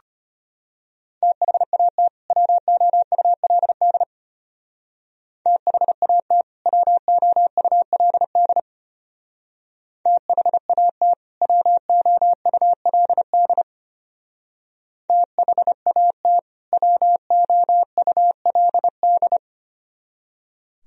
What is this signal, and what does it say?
Morse code, heard as a keyed tone, about 29 wpm, 700 Hz. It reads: THAT WOULD THAT WOULD THAT WOULD THAT WOULD
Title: Speed-Racing - Sets of 2 Words 25wpm